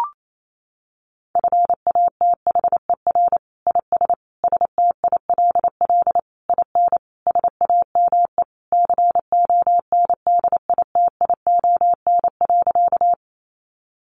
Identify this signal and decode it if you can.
Morse code, heard as a keyed tone, about 28 wpm, 700 Hz. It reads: FATHER IS STILL IN SAME CONDITION.
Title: Sentences from Top 1000 Words 28wpm